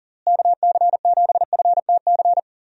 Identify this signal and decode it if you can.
Morse code, heard as a keyed tone, about 40 wpm, 700 Hz. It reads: KC7FTC
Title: USA Call Signs - 6 Characters - Repeated 3x - 40wpm